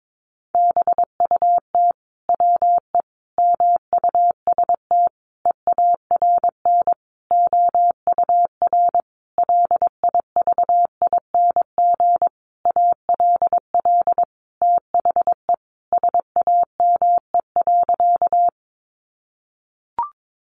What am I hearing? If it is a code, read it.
Morse code, heard as a keyed tone, about 22 wpm, 700 Hz. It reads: BUT WE MUST EARN OUR LIVING ALL THE SAME.
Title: Sentences from Top 1500 Words 22wpm